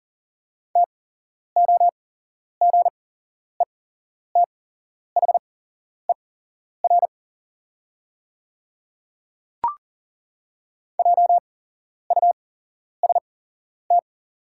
Morse code, heard as a keyed tone, about 40 wpm, spaced as for 11 wpm, 690 Hz. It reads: TOGETHER JUST
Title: 8x Character Spacing - Top 300 Words - 40wpm